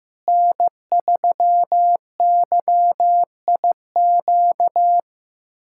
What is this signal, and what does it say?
Morse code, heard as a keyed tone, about 15 wpm, 700 Hz. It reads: N3YIQ
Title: USA Callsign DE Callsign - 5 Characters - 15wpm